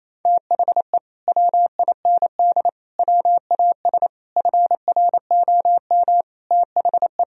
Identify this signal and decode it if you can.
Morse code, heard as a keyed tone, about 27 wpm, 700 Hz. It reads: THE WIND WAS FROM THE
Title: Sentences from Top 1000 Words - Review Part 2 - 28wpm